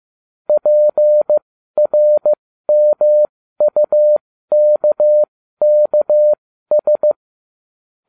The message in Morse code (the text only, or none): PRMUKKS